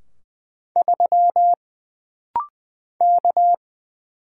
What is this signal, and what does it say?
Morse code, heard as a keyed tone, about 20 wpm, 700 Hz. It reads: 3 K
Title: Single Letter-Number - Rapid-Fire 20wpm